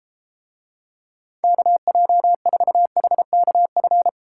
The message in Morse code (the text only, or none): KJ4HKF